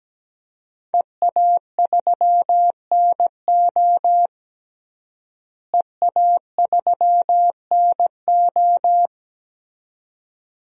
Morse code, heard as a keyed tone, about 17 wpm, 700 Hz. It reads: EA3NO EA3NO
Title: DXCC-100 Call Signs - 5 Characters - Repeated 2x - 17wpm